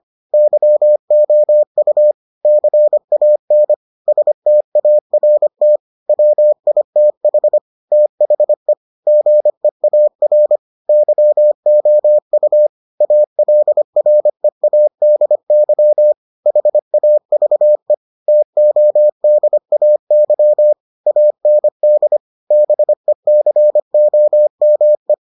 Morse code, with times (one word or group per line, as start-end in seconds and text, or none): 0.3-2.1 YOU
2.5-3.7 CAN
4.1-5.8 START
6.1-7.6 WITH
7.9-8.7 THE
9.1-10.6 GEAR
10.9-12.7 YOU
13.0-16.1 ALREADY
16.5-17.9 HAVE
18.3-20.7 TODAY
21.1-22.2 AND
22.5-25.1 BECOME